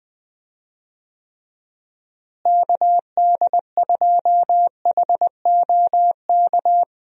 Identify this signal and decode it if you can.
Morse code, heard as a keyed tone, about 20 wpm, 700 Hz. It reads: KD2HOK
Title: USA Call Signs - 6 Characters - Repeated 2x - 20wpm